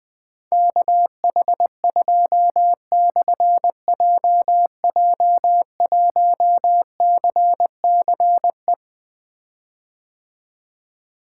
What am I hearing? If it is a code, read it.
Morse code, heard as a keyed tone, about 21 wpm, 700 Hz. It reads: KH2/JJ1CCE
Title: Difficult Call Signs 20wpm